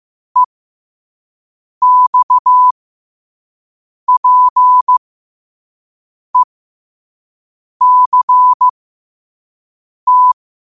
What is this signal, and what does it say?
Morse code, heard as a keyed tone, about 15 wpm, spaced as for 5 wpm, 1000 Hz. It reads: EXPECT
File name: qotdfiles / 5wpm